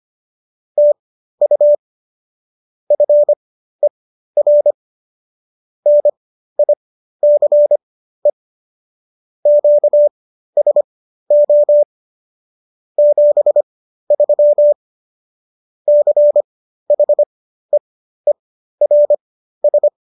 Morse code, heard as a keyed tone, about 25 wpm, spaced as for 13 wpm, 590 Hz. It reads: TU FER NICE QSO 73 CHEERS